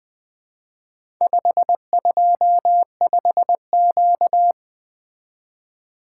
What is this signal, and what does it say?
Morse code, heard as a keyed tone, about 20 wpm, 700 Hz. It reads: 525Q
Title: Four Letter-Number Combinations - 20wpm